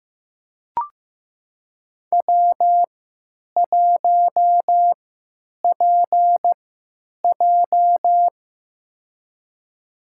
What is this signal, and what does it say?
Morse code, heard as a keyed tone, about 15 wpm, spaced as for 9 wpm, 700 Hz. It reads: W1PJ
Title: USA Call Signs - 4 Characters - 3x Character Spacing - 15wpm